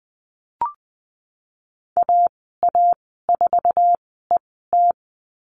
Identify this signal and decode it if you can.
Morse code, heard as a keyed tone, about 20 wpm, spaced as for 15 wpm, 700 Hz. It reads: AA4ET
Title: 154 Lesson 25g - Letter p - Call Signs 20wpm